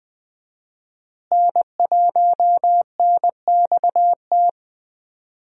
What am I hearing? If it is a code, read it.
Morse code, heard as a keyed tone, about 20 wpm, 700 Hz. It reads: N1NXT
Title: USA Call Signs - 5 Characters - Repeated 2x - 20wpm